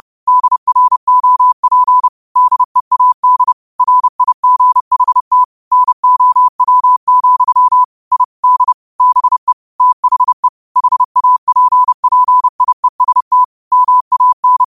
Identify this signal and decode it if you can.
Morse code, heard as a keyed tone, about 30 wpm, 1000 Hz. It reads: DROP DEAD RIGHT NOW, I D BE THE HAPPIEST MAN